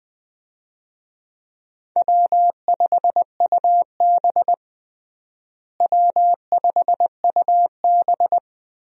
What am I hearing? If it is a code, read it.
Morse code, heard as a keyed tone, about 20 wpm, 700 Hz. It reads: W5UB W5UB
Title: USA Call Signs - 4 Characters - Repeated 2x - 20wpm